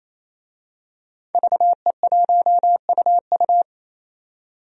Morse code, heard as a keyed tone, about 28 wpm, 700 Hz. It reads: VE1UU
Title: DXCC-100 Call Signs - 5 Characters - 28wpm